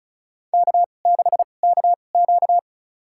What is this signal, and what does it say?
Morse code, heard as a keyed tone, about 36 wpm, spaced as for 25 wpm, 700 Hz. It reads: K6KQ
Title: USA Call Signs - 4 Characters - 2x Character Spacing - 35wpm